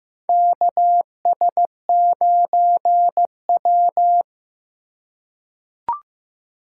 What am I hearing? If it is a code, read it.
Morse code, heard as a keyed tone, about 15 wpm, 700 Hz. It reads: KS9W
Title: K1USN SST QSO Practice - 15wpm